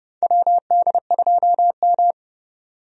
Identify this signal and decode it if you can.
Morse code, heard as a keyed tone, about 30 wpm, 700 Hz. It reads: WD2M
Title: USA Call Signs - 4 Characters - 30wpm